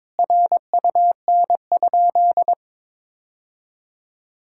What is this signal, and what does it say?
Morse code, heard as a keyed tone, about 22 wpm, 700 Hz. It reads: RUN?
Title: Sentences from Top 1000 Words - Review Part 6 - 22wpm